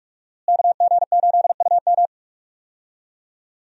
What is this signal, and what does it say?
Morse code, heard as a keyed tone, about 43 wpm, 700 Hz. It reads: KG8UM
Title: USA Call Signs - 5 Characters - Repeated 2x - 45wpm